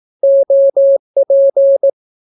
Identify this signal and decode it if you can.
Morse code, heard as a keyed tone, about 18 wpm, 550 Hz. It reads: OP
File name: naukapodstaw / skroty / skroty